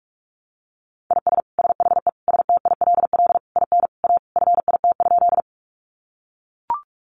Morse code, heard as a keyed tone, about 44 wpm, 700 Hz. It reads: IS SHE STILL IN A FIT?
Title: Sentences from Top 800 Words 45wpm